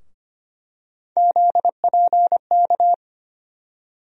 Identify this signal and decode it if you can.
Morse code, heard as a keyed tone, about 25 wpm, 700 Hz. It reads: ZPK
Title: USA Call Sign Postfixes - 3 Letters 25wpm